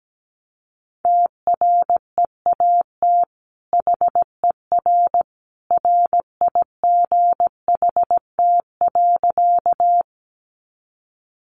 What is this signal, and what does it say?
Morse code, heard as a keyed tone, about 17 wpm, 700 Hz. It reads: TREAT HER RIGHT.
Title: Sentences from Top 1300 Words 17wpm